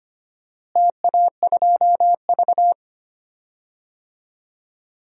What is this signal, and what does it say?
Morse code, heard as a keyed tone, about 25 wpm, 700 Hz. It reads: TA2V
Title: DXCC-100 Call Signs - 4 Characters - Repeated 3x - 25wpm